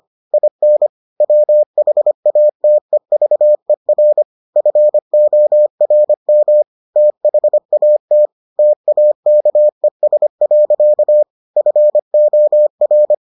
Morse code, heard as a keyed tone, about 25 wpm, 600 Hz. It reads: IN WHATEVER FORM THAT TAKES. FOR